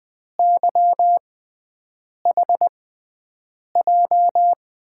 Morse code, heard as a keyed tone, about 20 wpm, spaced as for 7 wpm, 700 Hz. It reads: YHJ
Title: USA Call Signs - 6 Characters - 6x Character Spacing - 20wpm